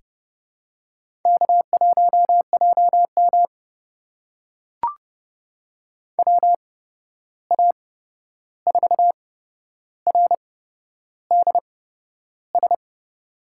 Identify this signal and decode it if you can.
Morse code, heard as a keyed tone, about 30 wpm, spaced as for 11 wpm, 700 Hz. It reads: K1JM WA4RDS
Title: USA Call Signs - 8x Character Spacing - 30wpm